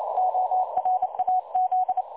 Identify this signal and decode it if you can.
Morse code, heard as a keyed tone, about 30 wpm, 740 Hz. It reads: ORAZ
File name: dlazaawansowanych / nagraniazpasma / websdr recording 3565.0kHz